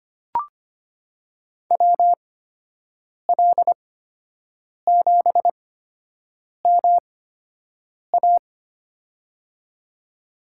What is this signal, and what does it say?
Morse code, heard as a keyed tone, about 25 wpm, spaced as for 7 wpm, 700 Hz. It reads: WL7MA
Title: USA Call Signs - 5 Characters - 8x Character Spacing - 25wpm